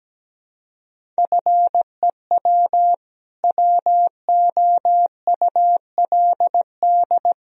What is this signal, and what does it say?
Morse code, heard as a keyed tone, about 17 wpm, 700 Hz. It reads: FEW WOULD
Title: Sentences from Top 1000 Words - Review Part 1 - 17wpm